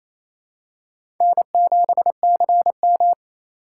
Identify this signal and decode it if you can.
Morse code, heard as a keyed tone, about 28 wpm, 700 Hz. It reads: N7CM
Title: USA Call Signs - 4x Character Spacing - 28wpm